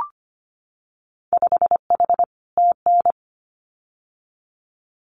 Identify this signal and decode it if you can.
Morse code, heard as a keyed tone, about 25 wpm, 700 Hz. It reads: 5H TN
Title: Winter Field Day Exchanges Encore 25wpm